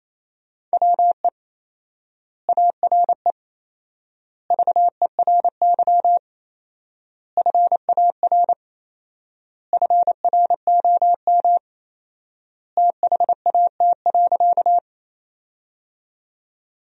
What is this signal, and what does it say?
Morse code, heard as a keyed tone, about 28 wpm, 700 Hz. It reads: WE ARE VERY FAR FROM THAT.
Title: 4x Word-Spacing - Sentences from Top 300 Words - 28wpm